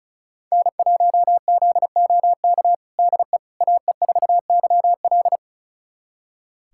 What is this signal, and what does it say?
Morse code, heard as a keyed tone, about 36 wpm, 700 Hz. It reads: N1ZOK DE AE4YL
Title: USA Callsign DE Callsign - 5 Characters - 35wpm